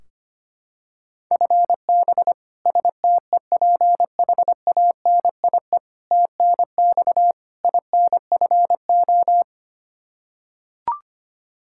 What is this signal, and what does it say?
Morse code, heard as a keyed tone, about 25 wpm, 700 Hz. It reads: FB STEPHANIE TNX INFO
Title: Practice QSO - Exchange 3 - More Info - 25wpm